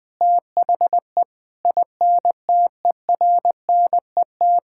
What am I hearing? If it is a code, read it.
Morse code, heard as a keyed tone, about 20 wpm, 700 Hz. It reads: THE INTERNET
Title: Sentences with 6-Word Phrases - 20wpm